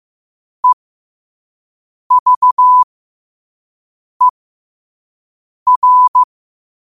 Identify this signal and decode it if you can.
Morse code, heard as a keyed tone, about 15 wpm, spaced as for 5 wpm, 1000 Hz. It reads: EVER
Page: Quotes for Wed, 13 Aug 2025 in Morse Code at 5 words per minute.